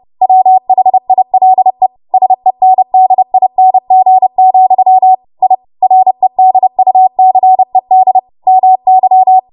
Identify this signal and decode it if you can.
Morse code, heard as a keyed tone, about 30 wpm, 750 Hz. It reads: WHILE SENDING, I REDUCED MY
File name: Morse / 30WPM